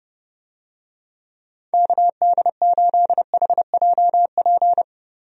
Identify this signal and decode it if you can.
Morse code, heard as a keyed tone, about 30 wpm, 700 Hz. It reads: KD8HJP
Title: USA Call Signs - 6 Characters - Repeated 2x - 30wpm